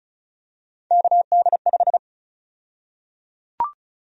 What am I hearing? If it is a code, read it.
Morse code, heard as a keyed tone, about 34 wpm, 700 Hz. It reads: KD5
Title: USA Call Sign Prefixes - 35wpm